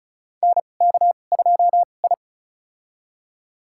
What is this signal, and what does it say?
Morse code, heard as a keyed tone, about 34 wpm, spaced as for 25 wpm, 700 Hz. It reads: NK2I
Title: USA Call Signs - 4 Characters - 2x Character Spacing - 35wpm